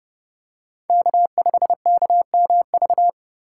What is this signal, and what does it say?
Morse code, heard as a keyed tone, about 30 wpm, 700 Hz. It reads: K5KMV
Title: USA Call Signs - 5 Characters - 2x Character Spacing - 30wpm